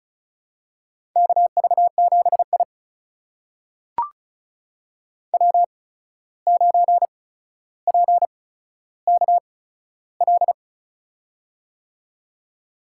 Morse code, heard as a keyed tone, about 36 wpm, spaced as for 14 wpm, 700 Hz. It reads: KV7I W9PKL
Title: USA Call Signs - 8x Character Spacing - 35wpm